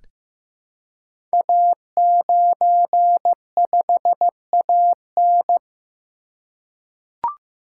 Morse code, heard as a keyed tone, about 15 wpm, 700 Hz. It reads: A95AN
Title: Code Groups - 5 Letter:Numbers - 15wpm